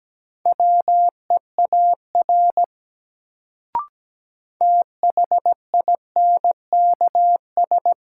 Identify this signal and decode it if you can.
Morse code, heard as a keyed tone, about 17 wpm, 700 Hz. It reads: WEAR THINKS
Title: Sentences from Top 1000 Words - Review Part 6 - New Word Drill - 17wpm